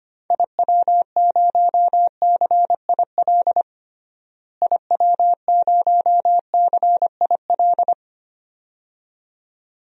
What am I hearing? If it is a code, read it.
Morse code, heard as a keyed tone, about 25 wpm, 700 Hz. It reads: IW0CIL IW0CIL
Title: DXCC-100 Call Signs - 6 Characters - Repeated 2x - 25wpm